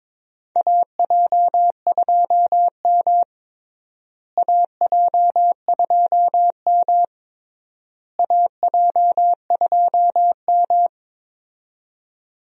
Morse code, heard as a keyed tone, about 22 wpm, 700 Hz. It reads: AJ2M AJ2M AJ2M
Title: USA Call Signs - 4 Characters - Repeated 3x - 22wpm